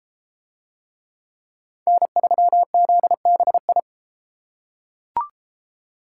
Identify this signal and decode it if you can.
Morse code, heard as a keyed tone, about 32 wpm, 700 Hz. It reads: N3ZBI